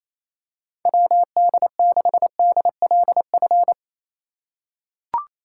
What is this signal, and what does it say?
Morse code, heard as a keyed tone, about 28 wpm, 700 Hz. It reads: WD6DLF
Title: USA Call Signs - 2x Character Spacing - 28wpm